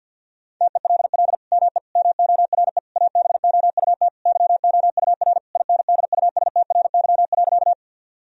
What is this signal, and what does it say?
Morse code, heard as a keyed tone, about 50 wpm, 700 Hz. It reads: TELL ME MORE ABOUT YOUR INDUSTRY.
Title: Sentences from Top 1000 Words - Review Part 4 - 50wpm